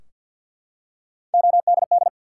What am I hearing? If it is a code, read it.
Morse code, heard as a keyed tone, about 52 wpm, 700 Hz. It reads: ODD
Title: Three Letter Words - 50wpm